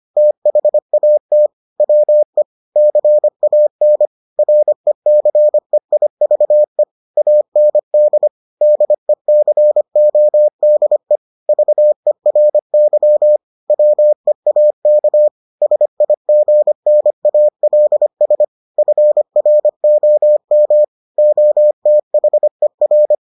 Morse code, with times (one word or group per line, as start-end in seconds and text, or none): 0.2-1.5 THAT
1.8-2.4 WE
2.8-4.1 CAN
4.4-6.8 RECEIVE
7.2-8.3 AND
8.6-11.2 DECODE
11.5-13.4 VERY
13.7-15.3 WEAK
15.6-18.4 SIGNALS
18.8-20.8 FROM
21.2-23.1 OTHER